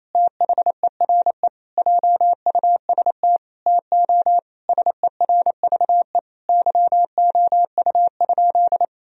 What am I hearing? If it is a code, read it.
Morse code, heard as a keyed tone, about 28 wpm, 700 Hz. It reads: THERE JUST TO SERVE YOU?
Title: Sentences from Top 1000 Words - Review Part 2 - 28wpm